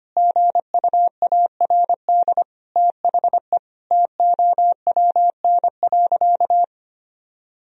Morse code, heard as a keyed tone, about 25 wpm, 700 Hz. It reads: GUARD THE TOWN.
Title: Sentences from Top 1000 Words - Review Part 4 - 25wpm